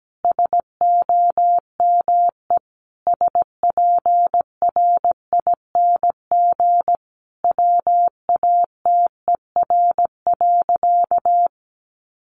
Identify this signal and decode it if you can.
Morse code, heard as a keyed tone, about 17 wpm, 700 Hz. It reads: SOME SPRING WATER.
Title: Sentences from Top 1200 Words 17wpm